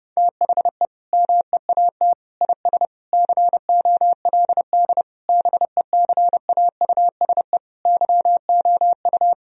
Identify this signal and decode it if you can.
Morse code, heard as a keyed tone, about 30 wpm, 700 Hz. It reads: THE MEAT IS COLD BECAUSE YOU
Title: Sentences from Top 1000 Words - Review Part 3 - 30wpm